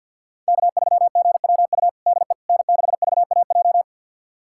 Morse code, heard as a keyed tone, about 53 wpm, 700 Hz. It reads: K3GWU DE N6VAJ
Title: USA Callsign DE Callsign - 5 Characters - 50wpm